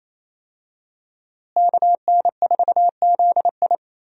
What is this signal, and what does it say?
Morse code, heard as a keyed tone, about 28 wpm, 700 Hz. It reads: KN4ZI
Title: USA Call Signs - 5 Characters - Repeated 2x - 28wpm